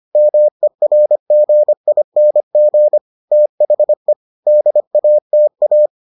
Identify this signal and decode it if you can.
Morse code, heard as a keyed tone, about 25 wpm, 600 Hz. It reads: MERGING THE DATA